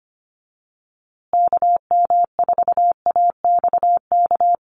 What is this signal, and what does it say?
Morse code, heard as a keyed tone, about 25 wpm, 700 Hz. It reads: KM4AXK
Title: USA Call Signs Repeated 3x 25wpm